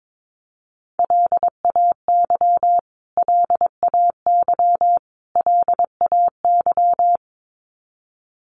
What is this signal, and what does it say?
Morse code, heard as a keyed tone, about 22 wpm, 700 Hz. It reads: LAY LAY LAY
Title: Top 500 Words Repeated 22wpm